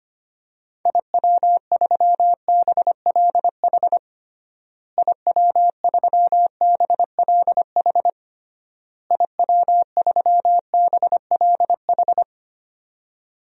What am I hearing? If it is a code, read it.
Morse code, heard as a keyed tone, about 25 wpm, 700 Hz. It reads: IW3BLH IW3BLH IW3BLH
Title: DXCC-100 Call Signs - 6 Characters - Repeated 3x - 25wpm